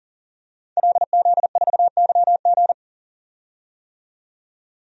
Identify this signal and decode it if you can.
Morse code, heard as a keyed tone, about 40 wpm, 700 Hz. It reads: LZ4YG
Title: DXCC-100 Call Signs - 5 Characters - 40wpm